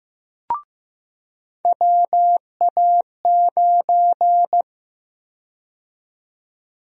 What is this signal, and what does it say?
Morse code, heard as a keyed tone, about 15 wpm, 700 Hz. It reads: WA9
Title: USA Call Sign Prefixes - 15wpm